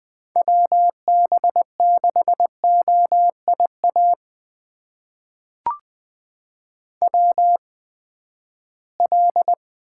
Morse code, heard as a keyed tone, about 20 wpm, spaced as for 14 wpm, 700 Hz. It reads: WB6OIA WL
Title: USA Call Signs - 6 Characters - 8x Character Spacing - 20wpm